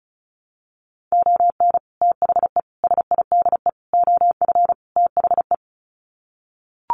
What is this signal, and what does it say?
Morse code, heard as a keyed tone, about 34 wpm, 700 Hz. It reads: ON THE SIDE OF THE
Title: Sets of 5 Words 35wpm